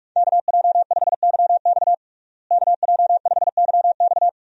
Morse code, heard as a keyed tone, about 46 wpm, 700 Hz. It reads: KJ5YX KJ5YX
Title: USA Call Signs - 5 Characters - Repeated 2x - 45wpm